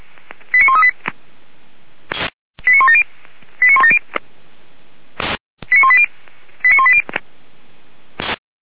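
Five-tone selective call (ZVEI), heard as quick runs of five tones, tones of about 70 ms. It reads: 89138 89170 89170 89180 89180
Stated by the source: ZVEI five-tone selective call received on SDR, logged by the Feuerwehr Orlamünde fire brigade server.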